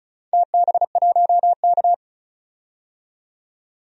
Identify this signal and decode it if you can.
Morse code, heard as a keyed tone, about 36 wpm, 700 Hz. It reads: TB1K
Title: DXCC-100 Call Signs - 4 Characters - 35wpm